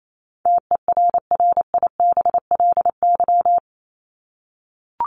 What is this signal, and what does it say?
Morse code, heard as a keyed tone, about 28 wpm, 700 Hz. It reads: TERRIBLY
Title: Top 1751-2000 Words 28wpm